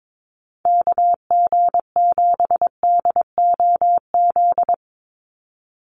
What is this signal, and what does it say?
Morse code, heard as a keyed tone, about 22 wpm, 700 Hz. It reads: KG7DOZ
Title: USA Call Signs 22wpm